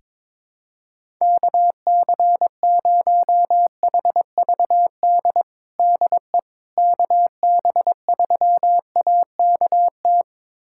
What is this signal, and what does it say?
Morse code, heard as a keyed tone, about 21 wpm, 700 Hz. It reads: KC0HVD DE KB3AKT
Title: USA Callsign DE Callsign 22wpm